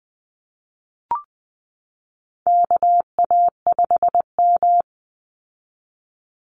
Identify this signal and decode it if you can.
Morse code, heard as a keyed tone, about 20 wpm, 700 Hz. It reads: KA5M
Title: Call Signs Repeated 3x - 20wpm